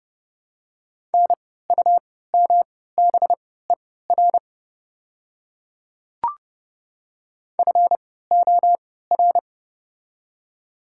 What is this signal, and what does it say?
Morse code, heard as a keyed tone, about 30 wpm, spaced as for 17 wpm, 700 Hz. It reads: NUMBER FOR
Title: 3x Character Spacing - Top 300 Words - 30wpm